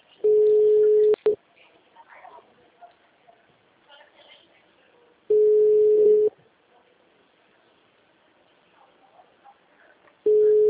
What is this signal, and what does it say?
Ringback tone (Europe, 425 Hz), with speech.